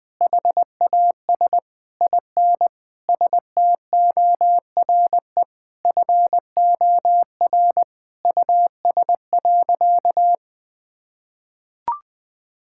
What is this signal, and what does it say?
Morse code, heard as a keyed tone, about 20 wpm, 700 Hz. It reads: HAS IN STORE FOR US.
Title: Sentences from Top 1000 Words - Review Part 5 - 20wpm